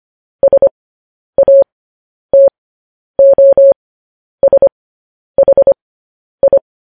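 Morse code, heard as a keyed tone, about 25 wpm, spaced as for 10 wpm, 550 Hz. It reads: SATOSHI
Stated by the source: Morse code transmission of news headlines.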